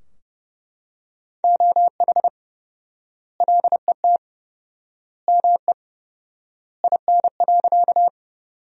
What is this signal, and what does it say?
Morse code, heard as a keyed tone, about 32 wpm, 700 Hz. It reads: OH LET ME IN.
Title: 4x Word-Spacing - Sentences from Top 500 Words - 30wpm